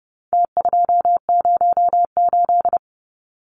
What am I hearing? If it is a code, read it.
Morse code, heard as a keyed tone, about 30 wpm, 700 Hz. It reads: T208
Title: Numbers - Contest Style - 4-Digit 30wpm